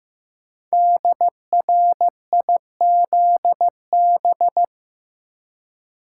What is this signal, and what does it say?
Morse code, heard as a keyed tone, about 15 wpm, 700 Hz. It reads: DRIZB
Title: Code Groups - 5 Letters - 15wpm